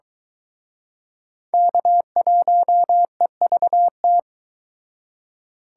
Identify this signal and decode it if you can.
Morse code, heard as a keyed tone, about 23 wpm, 700 Hz. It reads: K1EVT